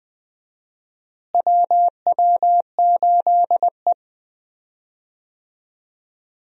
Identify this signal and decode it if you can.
Morse code, heard as a keyed tone, about 20 wpm, 700 Hz. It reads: WW8E
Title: USA Call Signs - 4 Characters - Repeated 3x - 20wpm